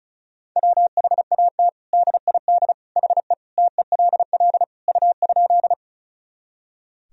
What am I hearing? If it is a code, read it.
Morse code, heard as a keyed tone, about 36 wpm, 700 Hz. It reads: WHAT DID HE TELL U?
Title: Sentences from Top 200 Words and Abbreviations 35wpm